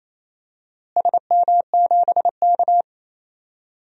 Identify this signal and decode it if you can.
Morse code, heard as a keyed tone, about 28 wpm, 700 Hz. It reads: SM7K
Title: DXCC-100 Call Signs - 4 Characters - Repeated 2x - 28wpm